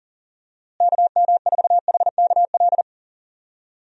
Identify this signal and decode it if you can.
Morse code, heard as a keyed tone, about 40 wpm, 700 Hz. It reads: KM4HKL
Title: USA Call Signs - 6 Characters - Repeated 2x - 40wpm